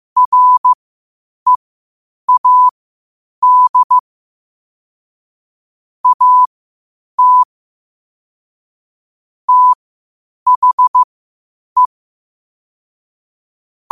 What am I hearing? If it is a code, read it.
Morse code, heard as a keyed tone, about 15 wpm, spaced as for 8 wpm, 1000 Hz. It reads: READ AT THE
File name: qotdfiles / 8wpm